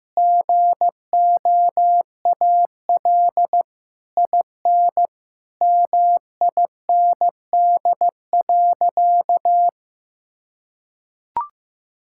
Morse code, heard as a keyed tone, about 15 wpm, 700 Hz. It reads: GOAL IN MIND.
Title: Sentences from Top 1000 Words - Review Part 5 - 15wpm